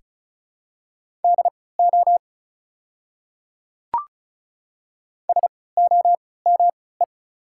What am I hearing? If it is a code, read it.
Morse code, heard as a keyed tone, about 34 wpm, spaced as for 20 wpm, 700 Hz. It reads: DO SOME
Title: 3x Character Spacing - Top 200 Words - 35wpm